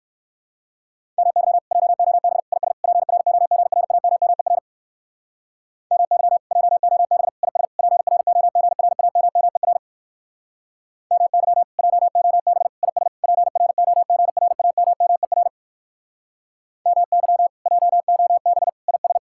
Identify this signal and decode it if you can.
Morse code, heard as a keyed tone, about 58 wpm, 700 Hz. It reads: MY JOB IS PROGRAMMER MY JOB IS PROGRAMMER MY JOB IS PROGRAMMER MY JOB IS